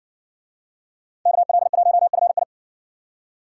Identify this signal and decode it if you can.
Morse code, heard as a keyed tone, about 60 wpm, 700 Hz. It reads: KL1FI